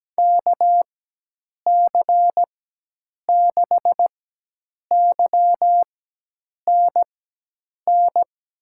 Morse code, heard as a keyed tone, about 17 wpm, spaced as for 8 wpm, 700 Hz. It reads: KC6YNN
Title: USA Call Signs - 6 Characters - 4x Character Spacing - 17wpm